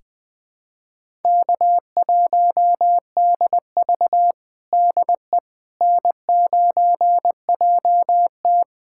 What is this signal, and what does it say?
Morse code, heard as a keyed tone, about 21 wpm, 700 Hz. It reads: K1DV DE N9JT
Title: USA Callsign DE Callsign - 4 Characters - 20wpm